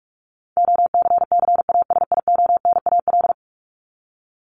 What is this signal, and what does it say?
Morse code, heard as a keyed tone, about 44 wpm, 700 Hz. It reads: OCCASIONAL
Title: LY Suffix - Top 100 Words with Base Words 45wpm